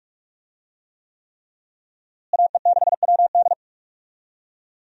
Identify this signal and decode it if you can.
Morse code, heard as a keyed tone, about 46 wpm, 700 Hz. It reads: AE6WD
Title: USA Call Signs - 5 Characters - Repeated 3x - 45wpm